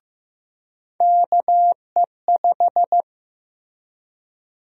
Morse code, heard as a keyed tone, about 15 wpm, 700 Hz. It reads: KE5
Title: USA Call Sign Prefixes - Repeated 2x - 15wpm